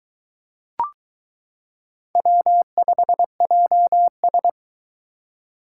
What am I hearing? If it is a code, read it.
Morse code, heard as a keyed tone, about 22 wpm, 700 Hz. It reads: W5JS